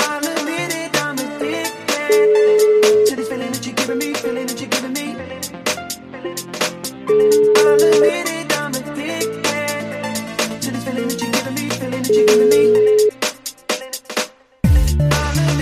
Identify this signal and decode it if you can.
Ringback tone (Europe, 425 Hz), with noise.